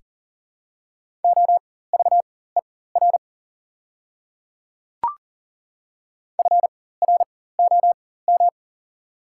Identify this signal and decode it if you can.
Morse code, heard as a keyed tone, about 40 wpm, spaced as for 19 wpm, 700 Hz. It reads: OVER FROM
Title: 4x Character Spacing - Top 500 Words - 40wpm